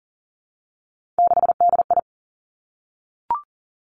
Morse code, heard as a keyed tone, about 40 wpm, 700 Hz. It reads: 6DI
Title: Three Challenging Letter-Number Combinations 40wpm